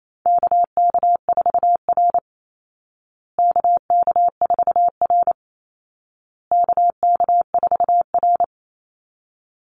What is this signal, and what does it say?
Morse code, heard as a keyed tone, about 28 wpm, 700 Hz. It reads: KK4R KK4R KK4R
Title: Call Signs Repeated 2x - 28wpm